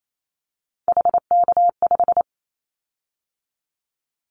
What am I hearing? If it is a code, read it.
Morse code, heard as a keyed tone, about 28 wpm, 700 Hz. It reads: HK5
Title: Callsign Prefixes for Top 100 DXCC - 28wpm